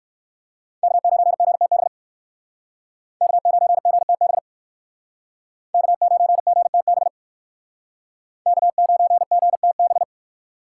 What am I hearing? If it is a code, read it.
Morse code, heard as a keyed tone, about 58 wpm, 700 Hz. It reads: K9GTB K9GTB K9GTB K9GTB